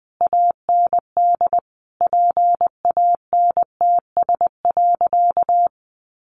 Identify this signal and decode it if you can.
Morse code, heard as a keyed tone, about 19 wpm, 700 Hz. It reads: AND PANTS.
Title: A Sound Of Thunder 20wpm